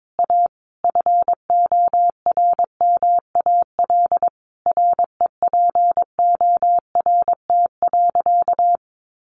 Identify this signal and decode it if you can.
Morse code, heard as a keyed tone, about 22 wpm, 700 Hz. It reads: A FORMAL REPORT.
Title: Sentences from Top 1300 Words 22wpm